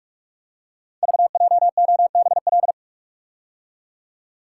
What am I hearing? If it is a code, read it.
Morse code, heard as a keyed tone, about 44 wpm, 690 Hz. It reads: VJOBL
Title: Code Groups - 5 Letters - 45wpm